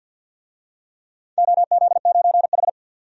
Morse code, heard as a keyed tone, about 51 wpm, 700 Hz. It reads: OZ9H
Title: DXCC-100 Call Signs - 4 Characters - 50wpm